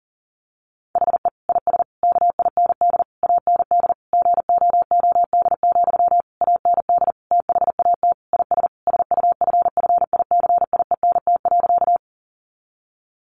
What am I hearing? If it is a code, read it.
Morse code, heard as a keyed tone, about 40 wpm, 700 Hz. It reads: HE IS KIND AND GOOD, AND THAT IS SUFFICIENT.
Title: Sentences from Top 1400 Words 40wpm